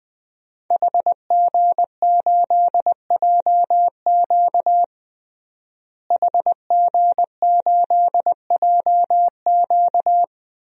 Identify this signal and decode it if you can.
Morse code, heard as a keyed tone, about 21 wpm, 700 Hz. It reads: HG8JQ HG8JQ
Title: DXCC-100 Call Signs - 5 Characters - Repeated 3x - 20wpm